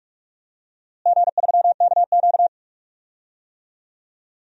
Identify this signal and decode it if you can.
Morse code, heard as a keyed tone, about 42 wpm, 700 Hz. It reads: G3KQ
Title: DXCC-100 Call Signs - 4 Characters - Repeated 2x - 45wpm